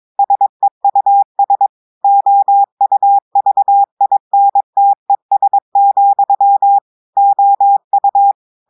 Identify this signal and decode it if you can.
Morse code, heard as a keyed tone, about 22 wpm, 800 Hz. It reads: SEUS OUVINTES, OU
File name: audios / portugais / portugais texte en clair 2 vitesse 22 mots minutes